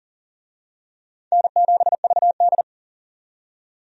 Morse code, heard as a keyed tone, about 40 wpm, 700 Hz. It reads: N7VD
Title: USA Call Signs - 4 Characters - 40wpm